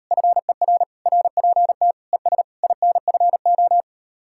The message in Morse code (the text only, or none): FER RPT ES INFO